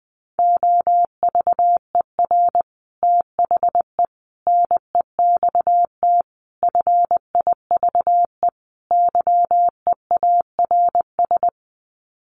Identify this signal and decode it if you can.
Morse code, heard as a keyed tone, about 20 wpm, 700 Hz. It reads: OVER THE NEXT FIVE YEARS
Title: Sets of 5 Words 20wpm